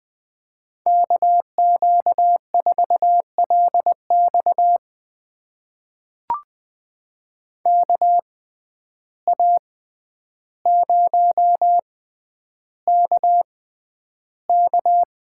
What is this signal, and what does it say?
Morse code, heard as a keyed tone, about 20 wpm, spaced as for 10 wpm, 700 Hz. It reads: KQ4LX KA0KK
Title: USA Call Signs - 6x Character Spacing - 20wpm